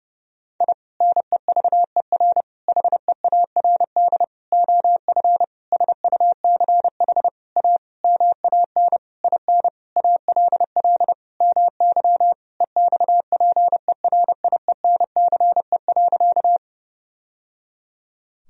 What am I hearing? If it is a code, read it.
Morse code, heard as a keyed tone, about 30 wpm, 700 Hz. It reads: I NEVER HEARD OF SUCH A MAN IN ALL MY EXPERIENCE.
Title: Sentences from Top 1000 Words - Review Part 3 - 30wpm